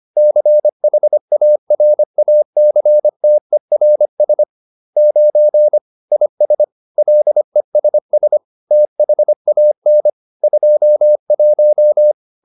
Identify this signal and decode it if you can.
Morse code, heard as a keyed tone, about 25 wpm, 600 Hz. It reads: CHARACTERS 9 IS LESS THAN 21
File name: SS220 25